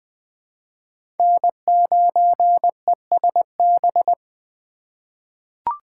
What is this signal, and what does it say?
Morse code, heard as a keyed tone, about 20 wpm, 700 Hz. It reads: N9ESB
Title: USA Call Signs - 2x Character Spacing - 20wpm